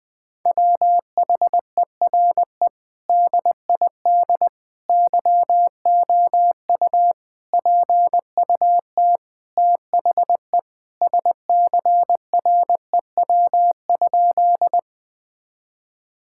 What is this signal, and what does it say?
Morse code, heard as a keyed tone, about 20 wpm, 700 Hz. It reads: WHERE DID YOU PUT THE SCREW?
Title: Sentences from Top 1600 Words 20wpm